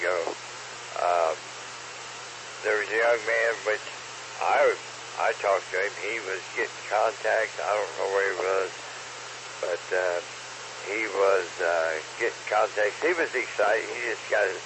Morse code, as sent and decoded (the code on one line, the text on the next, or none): none